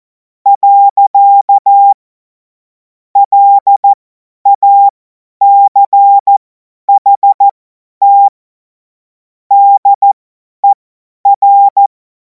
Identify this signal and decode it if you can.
Morse code, heard as a keyed tone, about 14 wpm, spaced as for 10 wpm, 800 Hz. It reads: . LACHT DER
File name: Morsekurs / Witze / Witz 30. Juni 2026 50 F70